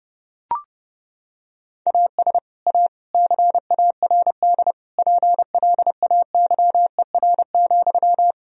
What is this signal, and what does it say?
Morse code, heard as a keyed tone, about 30 wpm, 700 Hz. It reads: AS A CARD PLAYER,
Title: Sentences from Top 1000 Words - Review Part 4 - 30wpm